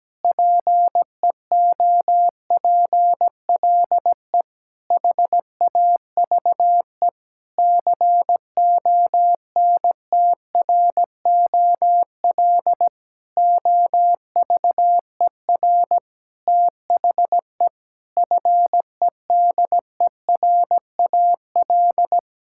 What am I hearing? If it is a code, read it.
Morse code, heard as a keyed tone, about 17 wpm, 700 Hz. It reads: PEOPLE HAVE CONTROL OVER THE FEDERAL
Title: Sentences from Top 1000 Words - Review Part 5 - 17wpm